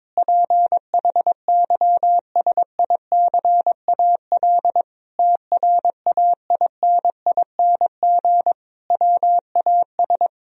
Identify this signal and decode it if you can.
Morse code, heard as a keyed tone, about 22 wpm, 700 Hz. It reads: PHYSICAL TRAINING WAS
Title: Sentences from Top 1000 Words - Review Part 4 - 22wpm